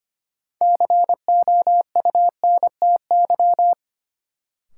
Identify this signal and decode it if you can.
Morse code, heard as a keyed tone, about 25 wpm, 700 Hz. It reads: COUNTY
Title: Sentences from Top 1200 Words - New Word Drill Alt - 25wpm